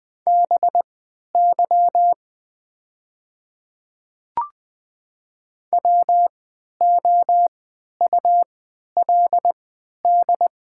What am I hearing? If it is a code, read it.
Morse code, heard as a keyed tone, about 20 wpm, spaced as for 11 wpm, 690 Hz. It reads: BY WOULD
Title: 3x Character Spacing - Top 100 Words - 20wpm